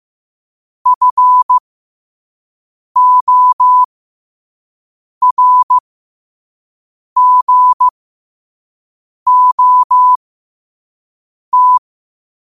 Morse code, heard as a keyed tone, about 15 wpm, spaced as for 5 wpm, 1000 Hz. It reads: FORGOT
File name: qotdfiles / 5wpm